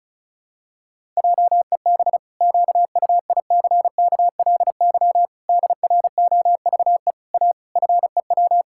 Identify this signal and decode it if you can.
Morse code, heard as a keyed tone, about 34 wpm, 700 Hz. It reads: JEB QUICKLY DROVE A FEW
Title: Pangram Challenge v2 - 35wpm